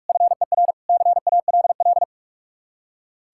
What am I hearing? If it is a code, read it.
Morse code, heard as a keyed tone, about 57 wpm, 700 Hz. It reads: FER CALL